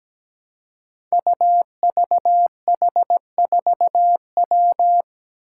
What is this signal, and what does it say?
Morse code, heard as a keyed tone, about 17 wpm, 700 Hz. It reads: UVH4W
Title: Five Challenging Letter-Number Combinations 17wpm